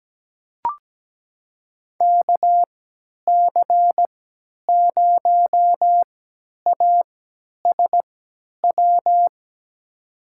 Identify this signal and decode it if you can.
Morse code, heard as a keyed tone, about 17 wpm, spaced as for 10 wpm, 700 Hz. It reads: KC0ASW
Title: USA Call Signs - 6 Characters - 3x Character Spacing - 17wpm